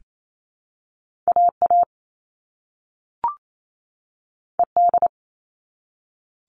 Morse code, heard as a keyed tone, about 28 wpm, 700 Hz. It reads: AA ED
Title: All Cut Numbers - 2-Digit 28wpm